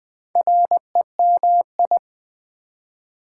Morse code, heard as a keyed tone, about 20 wpm, 700 Hz. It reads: REMI
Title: Build A Word - 2 Letters At A Time - 6-Letter Words - 20wpm